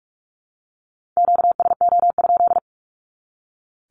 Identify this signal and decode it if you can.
Morse code, heard as a keyed tone, about 44 wpm, 700 Hz. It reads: QSO?
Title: Q-Codes 45wpm